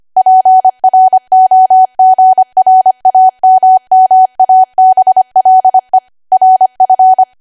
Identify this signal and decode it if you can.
Morse code, heard as a keyed tone, about 25 wpm, 750 Hz. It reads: PROGRAMMABLE RF